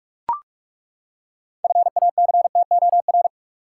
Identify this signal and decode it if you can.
Morse code, heard as a keyed tone, about 41 wpm, 700 Hz. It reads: FACTOR
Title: Six Letter Words - 45wpm